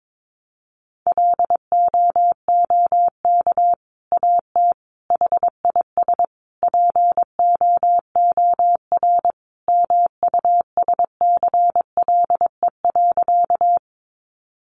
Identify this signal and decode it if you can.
Morse code, heard as a keyed tone, about 22 wpm, 700 Hz. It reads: LOOK AT HIS POOR MUSCLE.
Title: Sentences from Top 1300 Words 22wpm